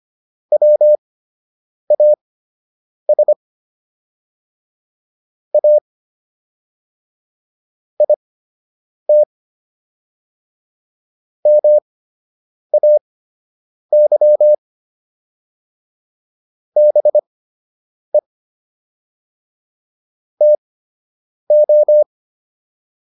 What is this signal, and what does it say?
Morse code, heard as a keyed tone, about 25 wpm, spaced as for 8 wpm, 600 Hz. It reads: WAS A IT MAY BE TO